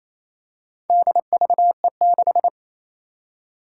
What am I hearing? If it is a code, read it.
Morse code, heard as a keyed tone, about 28 wpm, 700 Hz. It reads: DVE6
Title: Four Challenging Letter-Number Combinations 28wpm